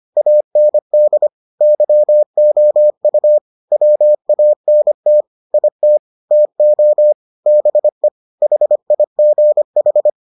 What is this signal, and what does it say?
Morse code, heard as a keyed tone, about 25 wpm, 600 Hz. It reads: AND YOU WANT IT TO BE HIGH